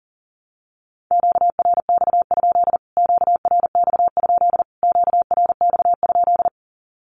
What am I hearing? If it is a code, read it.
Morse code, heard as a keyed tone, about 40 wpm, 700 Hz. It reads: QRX? QRX? QRX?
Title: Q-Codes Repeated 3x 40wpm